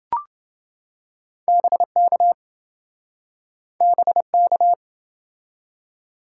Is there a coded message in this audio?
Yes: Morse code from 1.5 to 4.7 s, heard as a keyed tone, about 28 wpm, 700 Hz. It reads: BK BK